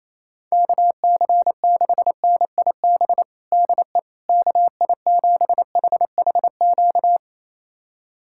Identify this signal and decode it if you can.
Morse code, heard as a keyed tone, about 28 wpm, 700 Hz. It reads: KC6NIB DE KI7HHQ
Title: USA Callsign DE Callsign - 6 Characters - 28wpm